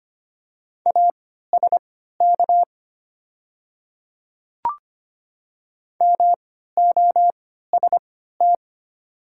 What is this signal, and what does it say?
Morse code, heard as a keyed tone, about 25 wpm, spaced as for 14 wpm, 700 Hz. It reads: ASK MOST
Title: 3x Character Spacing - Top 200 Words - 25wpm